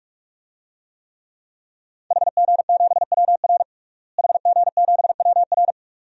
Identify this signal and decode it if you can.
Morse code, heard as a keyed tone, about 43 wpm, 700 Hz. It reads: HG7WR HG7WR
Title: DXCC-100 Call Signs - 5 Characters - Repeated 2x - 45wpm